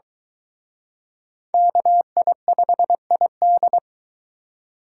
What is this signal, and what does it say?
Morse code, heard as a keyed tone, about 23 wpm, 700 Hz. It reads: KI5ID